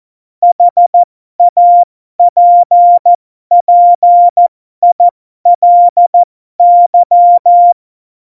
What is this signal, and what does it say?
Morse code, heard as a keyed tone, about 14 wpm, spaced as for 12 wpm, 700 Hz. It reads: HAPPILY
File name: Morsekurs / Witze